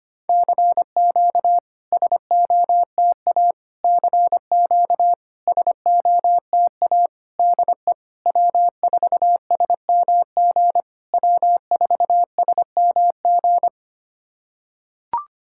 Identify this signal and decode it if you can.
Morse code, heard as a keyed tone, about 25 wpm, 700 Hz. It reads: CQ SOTA CQ SOTA DE W4SMG W4SMG
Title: SOTA QSO Practice 25wpm